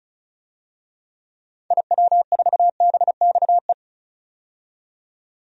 Morse code, heard as a keyed tone, about 36 wpm, 700 Hz. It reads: IW4BXE
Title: DXCC-100 Call Signs - 6 Characters - 35wpm